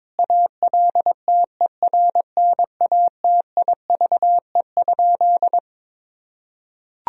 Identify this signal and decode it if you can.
Morse code, heard as a keyed tone, about 22 wpm, 700 Hz. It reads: ALTERNATIVE?
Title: Sentences from Top 1000 Words - Review Part 4 - 22wpm